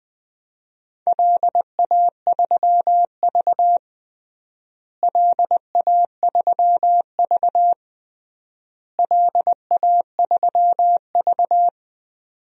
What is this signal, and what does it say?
Morse code, heard as a keyed tone, about 20 wpm, 700 Hz. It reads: LA3V LA3V LA3V
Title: DXCC-100 Call Signs - 4 Characters - Repeated 3x - 20wpm